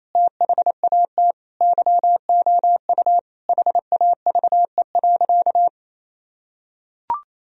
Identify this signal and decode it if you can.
Morse code, heard as a keyed tone, about 28 wpm, 700 Hz. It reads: THAT YOU HAVE.
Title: Sentences from Top 1000 Words - Review Part 4 - 28wpm